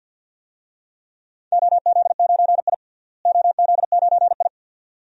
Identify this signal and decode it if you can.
Morse code, heard as a keyed tone, about 50 wpm, 700 Hz. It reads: OZ9I OZ9I
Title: DXCC-100 Call Signs - 4 Characters - Repeated 2x - 50wpm